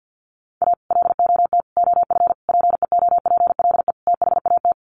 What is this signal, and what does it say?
Morse code, heard as a keyed tone, about 50 wpm, 700 Hz. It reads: A LOT OF PEOPLE THAT
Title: Sets of 5 Words Encore2 50wpm